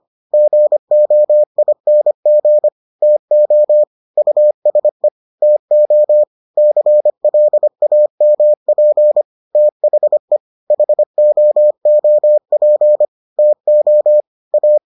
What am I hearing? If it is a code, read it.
Morse code, heard as a keyed tone, about 25 wpm, 600 Hz. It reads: GOING TO USE TO CLAMP THE HOOP TO A